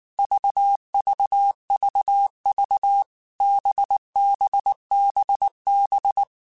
Morse code, heard as a keyed tone, about 19 wpm, 800 Hz. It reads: VVVV BBBB
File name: Basic Letter Groups By7Groups / VB46 / VB46@18